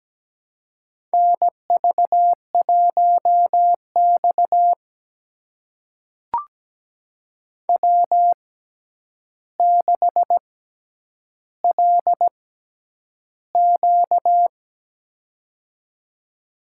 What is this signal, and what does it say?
Morse code, heard as a keyed tone, about 17 wpm, spaced as for 9 wpm, 700 Hz. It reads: NV1X W6LQ
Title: USA Call Signs - 4 Characters - 6x Character Spacing - 17wpm